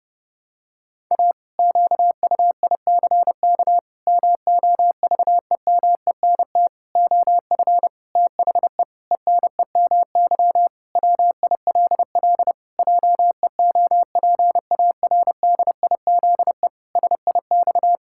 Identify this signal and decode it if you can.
Morse code, heard as a keyed tone, about 30 wpm, 700 Hz. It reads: A QUICK MOVEMENT OF THE ENEMY WILL JEOPARDIZE SIX
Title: Pangram Challenge v2 - 30wpm